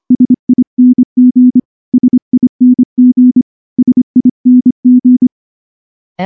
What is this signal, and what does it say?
Morse code, heard as a keyed tone, about 25 wpm, 250 Hz. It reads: SING SING SING
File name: morse / morse-training-25WPM